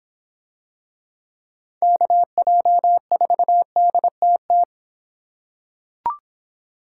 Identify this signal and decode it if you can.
Morse code, heard as a keyed tone, about 26 wpm, 700 Hz. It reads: KJ4DTT